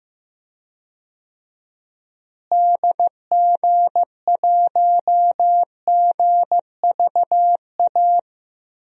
Morse code, heard as a keyed tone, about 15 wpm, 700 Hz. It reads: DG1GVA
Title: DXCC-100 Call Signs - 6 Characters - Repeated 2x - 15wpm